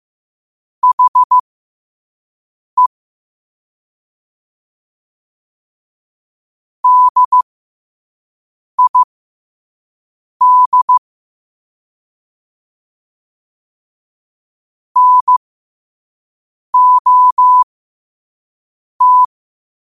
Morse code, heard as a keyed tone, about 15 wpm, spaced as for 5 wpm, 1000 Hz. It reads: HE DID NOT